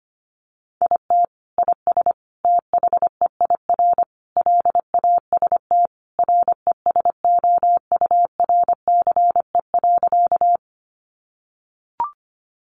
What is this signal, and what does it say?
Morse code, heard as a keyed tone, about 25 wpm, 700 Hz. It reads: IT IS THEIR LAST RESOURCE.
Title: Sentences from Top 1300 Words 25wpm